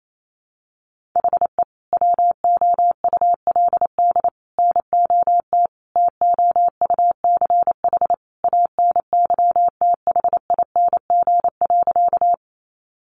Morse code, heard as a keyed tone, about 28 wpm, 700 Hz. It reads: HE WOULD NOT TOUCH ANYTHING.
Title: Sentences from Top 1200 Words 28wpm